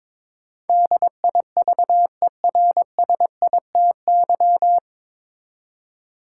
Morse code, heard as a keyed tone, about 22 wpm, 700 Hz. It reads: DIVERSITY
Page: Morse Code Ninja practice 22wpm